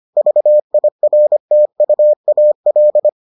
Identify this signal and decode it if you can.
Morse code, heard as a keyed tone, about 25 wpm, 600 Hz. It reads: VIRTUAL